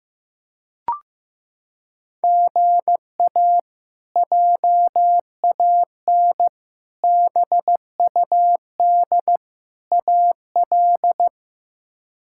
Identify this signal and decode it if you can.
Morse code, heard as a keyed tone, about 15 wpm, 700 Hz. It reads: GA JAN BUD AL
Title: K1USN SST QSO Practice - 15wpm